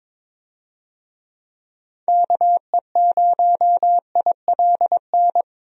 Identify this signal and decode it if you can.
Morse code, heard as a keyed tone, about 22 wpm, 700 Hz. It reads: KE0ILN
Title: USA Call Signs - 6 Characters - Repeated 2x - 22wpm